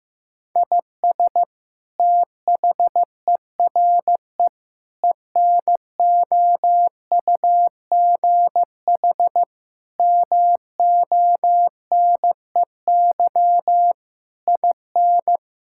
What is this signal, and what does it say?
Morse code, heard as a keyed tone, about 16 wpm, 700 Hz. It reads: IS THERE ENOUGH MONEY IN
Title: Sentences from Top 1000 Words - Review Part 6 - 15wpm